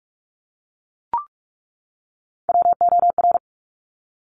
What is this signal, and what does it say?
Morse code, heard as a keyed tone, about 46 wpm, 690 Hz. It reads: WOR
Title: Top Three Letter Combinations in Words 45wpm